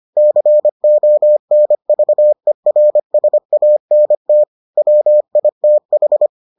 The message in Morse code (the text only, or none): CONVERSANT WITH